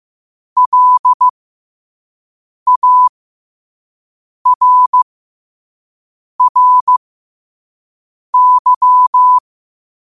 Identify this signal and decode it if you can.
Morse code, heard as a keyed tone, about 15 wpm, spaced as for 5 wpm, 1000 Hz. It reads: LARRY
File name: qotdfiles / 5wpm